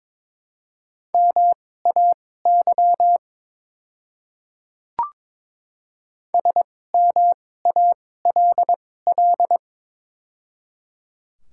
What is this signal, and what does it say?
Morse code, heard as a keyed tone, about 22 wpm, spaced as for 16 wpm, 700 Hz. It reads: MAY SMALL
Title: 2x Character Spacing - Top 300 Words - 22wpm